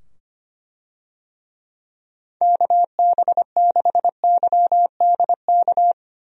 Morse code, heard as a keyed tone, about 25 wpm, 700 Hz. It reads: KB6YDK
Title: USA Call Signs - 6 Characters - 25wpm